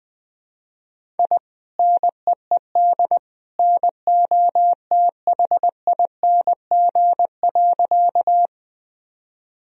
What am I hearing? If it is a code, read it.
Morse code, heard as a keyed tone, about 20 wpm, 700 Hz. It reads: I NEED NOTHING.
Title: Sentences from Top 1000 Words - Review Part 1 - 20wpm